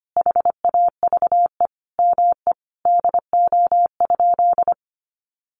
Morse code, heard as a keyed tone, about 25 wpm, 700 Hz. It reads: HAVE ME DO?
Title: Sentences from Top 200 Words 25wpm